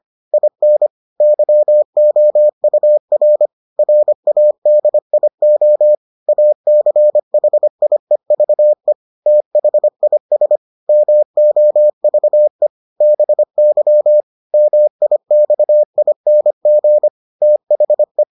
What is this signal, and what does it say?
Morse code, heard as a keyed tone, about 25 wpm, 600 Hz. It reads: IN YOUR RADIO ACHIEVE THIS MOVE BY MIXING THE